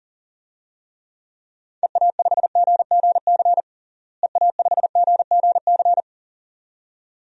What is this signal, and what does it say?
Morse code, heard as a keyed tone, about 40 wpm, 700 Hz. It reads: EA5GGC EA5GGC
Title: DXCC-100 Call Signs - 6 Characters - Repeated 2x - 40wpm